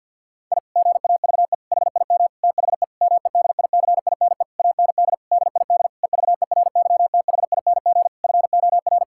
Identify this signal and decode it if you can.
Morse code, heard as a keyed tone, about 50 wpm, 700 Hz. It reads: I GAVE HIM THE MEDICINE AND DID EVERYTHING FOR
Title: Sentences from Top 1000 Words - Review Part 6 - 50wpm